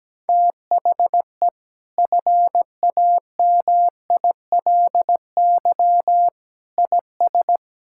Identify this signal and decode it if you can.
Morse code, heard as a keyed tone, about 17 wpm, 700 Hz. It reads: THE FAMILY IS
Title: Sentences from Top 1000 Words - Review Part 5 - 17wpm